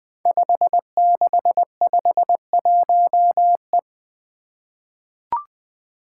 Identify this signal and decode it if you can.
Morse code, heard as a keyed tone, about 20 wpm, 700 Hz. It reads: 5651E
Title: Five Challenging Letter-Number Combinations 20wpm